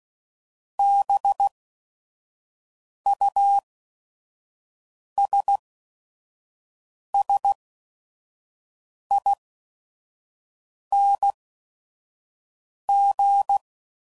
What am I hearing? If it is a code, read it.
Morse code, heard as a keyed tone, about 16 wpm, spaced as for 5 wpm, 800 Hz. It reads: BUSSING
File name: Suffixes / Special ING's / VerySlowToFaster ING's